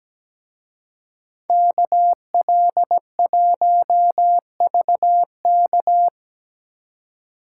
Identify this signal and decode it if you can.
Morse code, heard as a keyed tone, about 20 wpm, 700 Hz. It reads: KL1VK
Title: Speed Racing - USA 2x2 Call Signs 17wpm